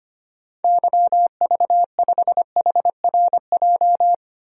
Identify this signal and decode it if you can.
Morse code, heard as a keyed tone, about 25 wpm, 700 Hz. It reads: YV5HRJ
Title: DXCC-100 Call Signs - 6 Characters - 25wpm